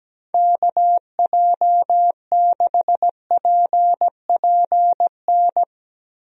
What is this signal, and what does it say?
Morse code, heard as a keyed tone, about 17 wpm, 700 Hz. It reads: KJ6PPN
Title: USA Call Signs - 6 Characters - Repeated 2x - 17wpm